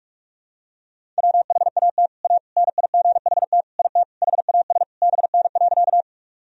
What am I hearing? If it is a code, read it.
Morse code, heard as a keyed tone, about 46 wpm, 700 Hz. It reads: WHAT A NIGHT IT HAS BN.
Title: Sentences from Top 300 Words and Abbreviations 45wpm